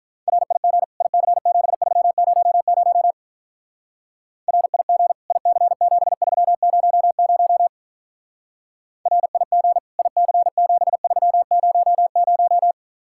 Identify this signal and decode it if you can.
Morse code, heard as a keyed tone, about 45 wpm, 700 Hz. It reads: RIG IC7300 RIG IC7300 RIG IC7300